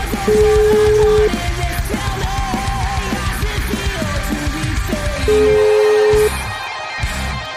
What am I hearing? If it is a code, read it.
Ringback tone (Europe, 425 Hz), with noise.